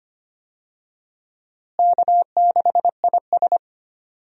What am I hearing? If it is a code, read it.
Morse code, heard as a keyed tone, about 25 wpm, 700 Hz. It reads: K6IS
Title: USA Call Signs - 4 Characters - Repeated 3x - 25wpm